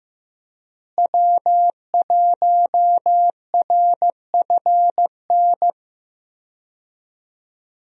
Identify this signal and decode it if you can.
Morse code, heard as a keyed tone, about 15 wpm, 700 Hz. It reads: W1RFN
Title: USA Call Signs - 5 Characters - Repeated 3x - 15wpm